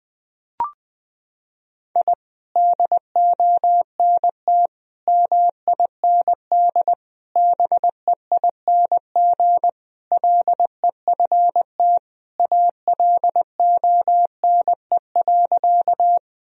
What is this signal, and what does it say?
Morse code, heard as a keyed tone, about 20 wpm, 700 Hz. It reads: I DONT MIND BEING LEFT ALONE.
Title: Sentences from Top 1000 Words - Review Part 3 - 20wpm